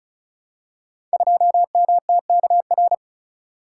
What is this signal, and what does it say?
Morse code, heard as a keyed tone, about 36 wpm, 700 Hz. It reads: 2MTKR
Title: Code Groups - 5 Letter:Numbers - 35wpm